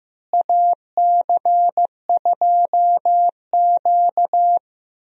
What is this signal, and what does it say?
Morse code, heard as a keyed tone, about 15 wpm, 700 Hz. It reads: AC2Q
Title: USA Call Signs - 4 Characters - Repeated 2x - 15wpm